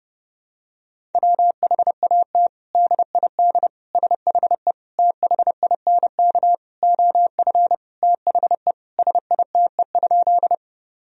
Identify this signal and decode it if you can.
Morse code, heard as a keyed tone, about 30 wpm, 700 Hz. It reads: WHAT DID SHE THINK OF THE SITE?
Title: Sentences from Top 1000 Words - Review Part 4 - 30wpm